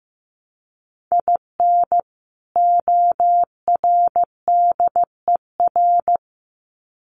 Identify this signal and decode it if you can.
Morse code, heard as a keyed tone, about 15 wpm, 700 Hz. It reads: IN ORDER
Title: Sets of 2 Words 15wpm